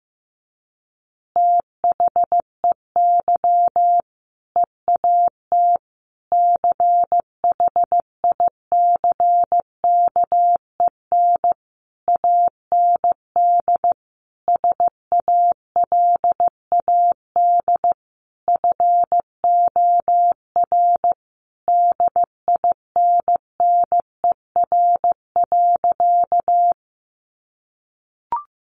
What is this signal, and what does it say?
Morse code, heard as a keyed tone, about 15 wpm, 700 Hz. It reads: THEY EAT CHICKEN AND SALAD FOR DINNER.
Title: Easy Phrases 15wpm